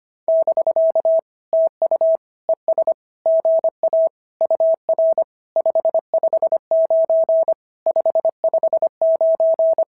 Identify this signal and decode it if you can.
Morse code, heard as a keyed tone, about 24 wpm, 650 Hz. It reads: <BK> TU ES GA UR 559 559
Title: SOTA QSO Practice 25wpm